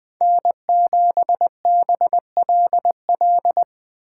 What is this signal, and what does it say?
Morse code, heard as a keyed tone, about 20 wpm, 700 Hz. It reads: N7BLL
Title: USA Call Signs - 5 Characters - 2x Character Spacing - 20wpm